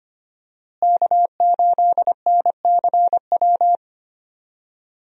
Morse code, heard as a keyed tone, about 25 wpm, 700 Hz. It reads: K8NCW
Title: USA Call Signs - 5 Characters - Repeated 2x - 25wpm